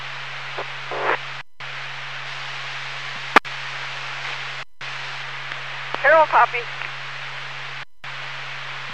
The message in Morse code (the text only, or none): none